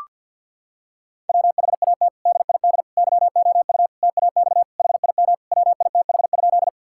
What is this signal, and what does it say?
Morse code, heard as a keyed tone, about 50 wpm, 700 Hz. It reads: WHAT DID YOU TAX HIM WITH?
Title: Sentences from Top 1000 Words - Review Part 5 - 50wpm